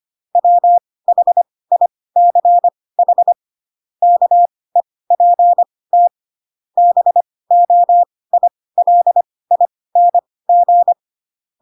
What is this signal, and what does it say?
Morse code, heard as a keyed tone, about 25 wpm, spaced as for 18 wpm, 700 Hz. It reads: WHICH KEPT BOILING
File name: Pinocchio ch3 12 25 ik1hge